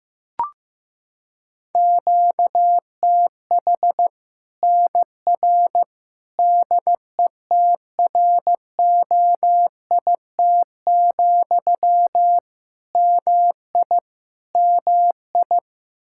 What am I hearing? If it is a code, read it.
Morse code, heard as a keyed tone, about 15 wpm, 700 Hz. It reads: QTH NR DETROIT, MI MI
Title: Practice QSO - Exchange 1 - Beginning - 15wpm